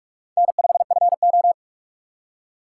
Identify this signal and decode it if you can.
Morse code, heard as a keyed tone, about 44 wpm, 700 Hz. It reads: N5FO
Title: USA Call Signs - 4x Character Spacing - 45wpm